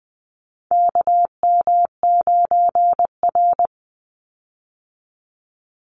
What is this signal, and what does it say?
Morse code, heard as a keyed tone, about 20 wpm, 700 Hz. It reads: KM9R
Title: Call Signs - 20wpm